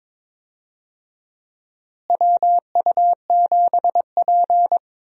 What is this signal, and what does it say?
Morse code, heard as a keyed tone, about 23 wpm, 700 Hz. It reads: WU7P
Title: USA Call Signs - 4 Characters - Repeated 2x - 22wpm